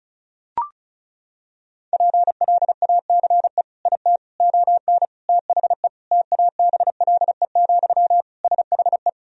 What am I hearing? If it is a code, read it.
Morse code, heard as a keyed tone, about 34 wpm, 700 Hz. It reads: PLACE IT ON THE TABLE, SHE
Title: Sentences from Top 1000 Words - Review Part 2 - 35wpm